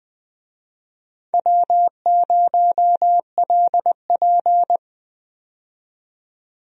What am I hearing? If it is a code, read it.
Morse code, heard as a keyed tone, about 20 wpm, 700 Hz. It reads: W0LP
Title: USA Call Signs - 4 Characters - 20wpm